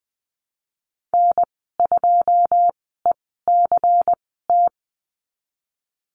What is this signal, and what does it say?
Morse code, heard as a keyed tone, about 19 wpm, spaced as for 14 wpm, 700 Hz. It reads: N2ECT
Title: 122 Lesson 20g - Number 3 - Call Signs 20wpm